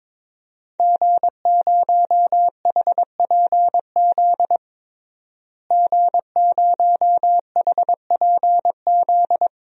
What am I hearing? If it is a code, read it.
Morse code, heard as a keyed tone, about 22 wpm, 700 Hz. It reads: G0HPZ G0HPZ
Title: DXCC-100 Call Signs - 5 Characters - Repeated 2x - 22wpm